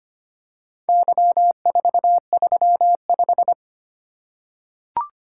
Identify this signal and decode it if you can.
Morse code, heard as a keyed tone, about 25 wpm, 690 Hz. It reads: Y435
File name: practice / 25wpm / Four Letter-Number Combinations - 25wpm